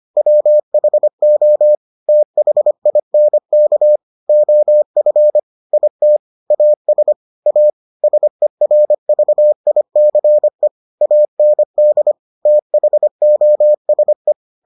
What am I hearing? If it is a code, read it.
Morse code, heard as a keyed tone, about 25 wpm, 600 Hz. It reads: WHO THINK OF IT AS A SERVICE AND THOSE